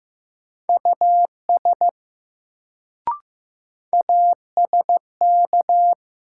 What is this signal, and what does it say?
Morse code, heard as a keyed tone, about 15 wpm, 700 Hz. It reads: US ASK
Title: Sentences from Top 200 Words - New Word Drill - 15wpm